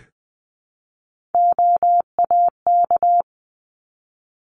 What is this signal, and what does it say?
Morse code, heard as a keyed tone, about 20 wpm, 700 Hz. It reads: OAK
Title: 260 Lesson 42e - Words 20-20wpm